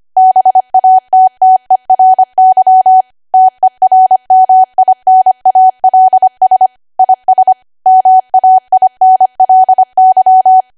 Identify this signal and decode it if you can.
Morse code, heard as a keyed tone, about 25 wpm, 750 Hz. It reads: BATTERY TERMINALS IS MAINLY
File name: Archive / 25 WPM / 25WPM